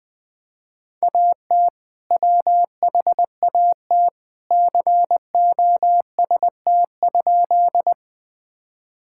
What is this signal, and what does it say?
Morse code, heard as a keyed tone, about 20 wpm, 700 Hz. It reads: AT WHAT COST?
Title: Sentences from Top 1000 Words - Review Part 5 - 20wpm